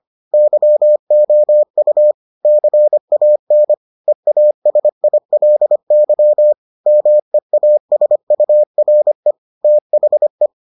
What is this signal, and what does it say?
Morse code, heard as a keyed tone, about 25 wpm, 600 Hz. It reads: YOU CAN EASILY MEASURE THE